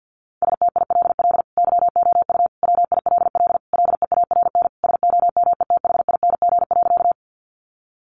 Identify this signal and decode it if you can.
Morse code, heard as a keyed tone, about 50 wpm, 700 Hz. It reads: STILL YOU WILL LEARN SOMETHING.
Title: Sentences from Top 600 Words 50wpm